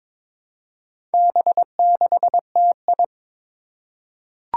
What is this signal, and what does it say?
Morse code, heard as a keyed tone, about 22 wpm, 700 Hz. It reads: B6TI
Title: Four Challenging Letter-Number Combinations 22wpm